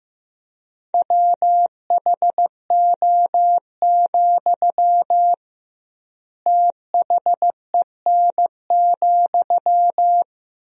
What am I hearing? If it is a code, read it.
Morse code, heard as a keyed tone, about 15 wpm, 700 Hz. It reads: WHO, THEN,
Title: 2x Word-Spacing - Sentences from Top 100 Words - 15wpm